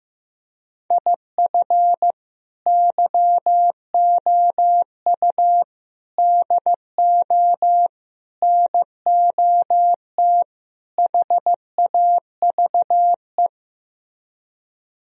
Morse code, heard as a keyed tone, about 15 wpm, 700 Hz. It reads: IF YOU DO NOT HAVE
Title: Speed-Racing - Sets of 5 Words 15wpm